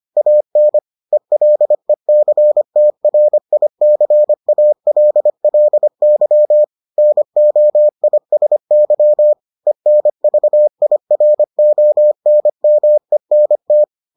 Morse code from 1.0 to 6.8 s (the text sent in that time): ELECTRICALLY